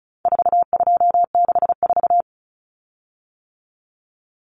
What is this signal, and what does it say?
Morse code, heard as a keyed tone, about 34 wpm, 700 Hz. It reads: 4264
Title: Common Cut Numbers - 4-Digit 35wpm